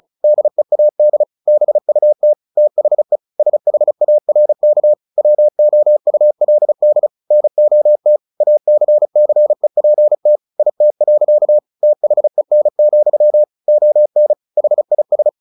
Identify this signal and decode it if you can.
Morse code, heard as a keyed tone, about 34 wpm, 600 Hz. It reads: DEAD BUT THE SHARK WOULD NOT ACCEPT IT. THEN, ON HIS
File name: the old man and the sea 2 35wpm